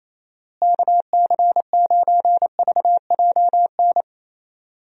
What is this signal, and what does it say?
Morse code, heard as a keyed tone, about 28 wpm, 700 Hz. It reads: KC9VJN
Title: USA Call Signs - 6 Characters - Repeated 2x - 28wpm